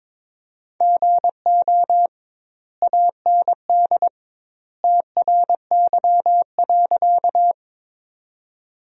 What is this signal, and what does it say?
Morse code, heard as a keyed tone, about 21 wpm, 700 Hz. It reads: GO AND TRY.
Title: 2x Word-Spacing - Sentences from Top 200 Words - 22wpm